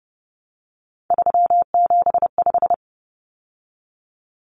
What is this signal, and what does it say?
Morse code, heard as a keyed tone, about 30 wpm, 700 Hz. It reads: 375
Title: Common Cut Numbers - 3-Digit 30wpm